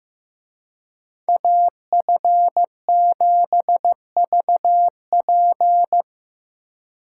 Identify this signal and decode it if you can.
Morse code, heard as a keyed tone, about 17 wpm, 700 Hz. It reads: AF7VP
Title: Speed Racing - USA 2x2 Call Signs 15wpm